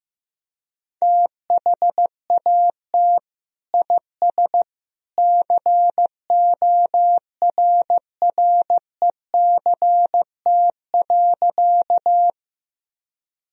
Morse code, heard as a keyed tone, about 15 wpm, 700 Hz. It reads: THAT IS CORRECT.
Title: Sentences from Top 1000 Words - Review Part 2 - 15wpm